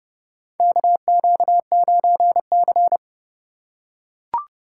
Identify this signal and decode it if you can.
Morse code, heard as a keyed tone, about 32 wpm, 700 Hz. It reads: KQ9C
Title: USA Call Signs - 4 Characters - 2x Character Spacing - 30wpm